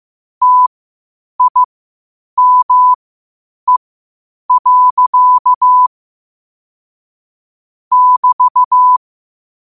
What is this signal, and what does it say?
Morse code, heard as a keyed tone, about 15 wpm, spaced as for 8 wpm, 1000 Hz. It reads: TIME. <BT>
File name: qotdfiles / 8wpm